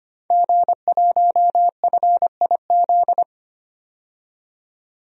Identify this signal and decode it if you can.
Morse code, heard as a keyed tone, about 25 wpm, 700 Hz. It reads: G1FIZ
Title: DXCC-100 Call Signs - 5 Characters - 25wpm